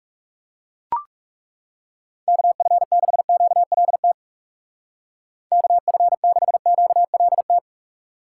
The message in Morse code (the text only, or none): KF6QLT KF6QLT